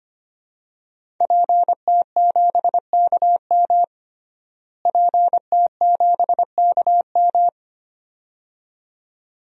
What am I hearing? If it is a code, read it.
Morse code, heard as a keyed tone, about 25 wpm, 700 Hz. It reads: PT7KM PT7KM
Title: DXCC-100 Call Signs - 5 Characters - Repeated 2x - 25wpm